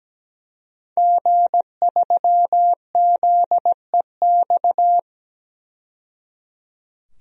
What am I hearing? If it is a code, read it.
Morse code, heard as a keyed tone, about 17 wpm, 700 Hz. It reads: G3ZEX
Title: DXCC-100 Call Signs - 5 Characters - 17wpm